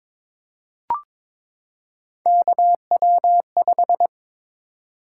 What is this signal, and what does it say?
Morse code, heard as a keyed tone, about 23 wpm, 700 Hz. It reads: KW5
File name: practice / 22wpm / Three Letter-Number Combinations - 22wpm